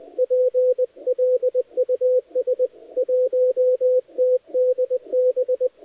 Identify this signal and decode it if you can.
Morse code, heard as a keyed tone, about 20 wpm, 510 Hz, with radio noise. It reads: PLUS 1TDB
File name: dlazaawansowanych / nagraniazpasma / titawaNETcontrol